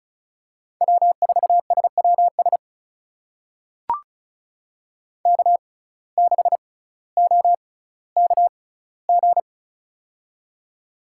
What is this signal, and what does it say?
Morse code, heard as a keyed tone, about 34 wpm, spaced as for 18 wpm, 700 Hz. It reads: W4SWS K6OKG
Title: USA Call Signs - 5 Characters - 6x Character Spacing - 35wpm